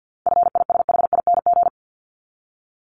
Morse code, heard as a keyed tone, about 50 wpm, 700 Hz. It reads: FISHING
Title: Top 751-1000 Words 50wpm